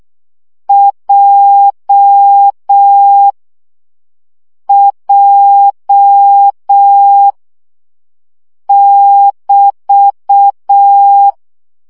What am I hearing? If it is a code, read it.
Morse code, heard as a keyed tone, about 6 wpm, 800 Hz. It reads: J J <BT>
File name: CW gyakorlas j 5 baud